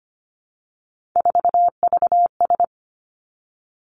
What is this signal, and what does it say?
Morse code, heard as a keyed tone, about 25 wpm, 700 Hz. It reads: 4VS
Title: Three Challenging Letter-Number Combinations 25wpm